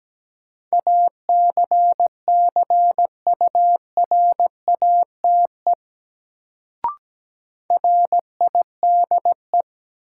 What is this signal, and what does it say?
Morse code, heard as a keyed tone, about 17 wpm, 700 Hz. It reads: ACCURATE RIDE
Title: Sentences from Top 1200 Words - New Word Drill Alt - 17wpm